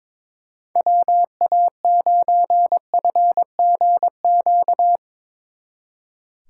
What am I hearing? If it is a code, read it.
Morse code, heard as a keyed tone, about 22 wpm, 700 Hz. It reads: WA9FGQ
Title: Speed Racing - USA 2x3 Call Signs 22wpm